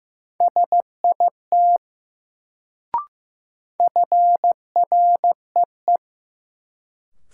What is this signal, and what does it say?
Morse code, heard as a keyed tone, about 15 wpm, 700 Hz. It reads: SIT FREE
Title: Sentences from Top 500 Words - New Word Drill - 15wpm